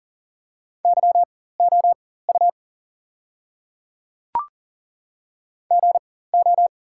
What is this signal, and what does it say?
Morse code, heard as a keyed tone, about 40 wpm, spaced as for 19 wpm, 690 Hz. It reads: YOU GO
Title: 4x Character Spacing - Top 200 Words - 40wpm